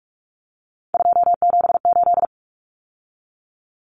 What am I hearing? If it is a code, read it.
Morse code, heard as a keyed tone, about 44 wpm, 700 Hz. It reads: 278
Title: Common Cut Numbers - 3-Digit 45wpm